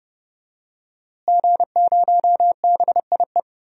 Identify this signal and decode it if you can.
Morse code, heard as a keyed tone, about 30 wpm, 700 Hz. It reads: G0BIE
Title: DXCC-100 Call Signs - 5 Characters - Repeated 3x - 30wpm